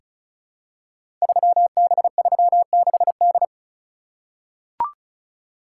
Morse code, heard as a keyed tone, about 36 wpm, 700 Hz. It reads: 3B36D
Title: Code Groups - 5 Letter:Numbers - 35wpm